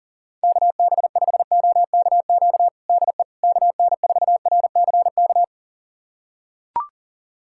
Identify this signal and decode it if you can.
Morse code, heard as a keyed tone, about 40 wpm, 700 Hz. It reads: KB5OKQ DE KN4RCK
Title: USA Callsign DE Callsign 40wpm